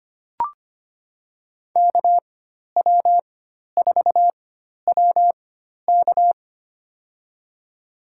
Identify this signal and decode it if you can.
Morse code, heard as a keyed tone, about 25 wpm, spaced as for 12 wpm, 700 Hz. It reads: KW4WK
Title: USA Call Signs - 5 Characters - 4x Character Spacing - 25wpm